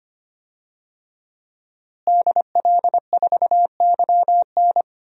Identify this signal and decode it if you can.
Morse code, heard as a keyed tone, about 25 wpm, 700 Hz. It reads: DL4YN
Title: DXCC-100 Call Signs - 5 Characters - Repeated 2x - 25wpm